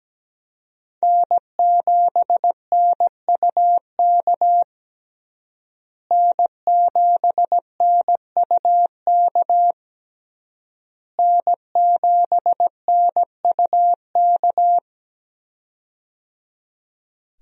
Morse code, heard as a keyed tone, about 17 wpm, 700 Hz. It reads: N7NUK N7NUK N7NUK
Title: USA Call Signs - 5 Characters - Repeated 3x - 17wpm